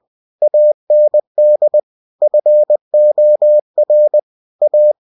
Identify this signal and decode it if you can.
Morse code, heard as a keyed tone, about 20 wpm, 600 Hz. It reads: AND FOR A
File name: cw-books / winnie-the-pooh / winnie-20wpm-page-0064